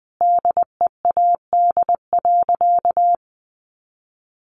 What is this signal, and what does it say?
Morse code, heard as a keyed tone, about 20 wpm, 700 Hz. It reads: DEAD.
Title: Sentences from Top 1500 Words 20wpm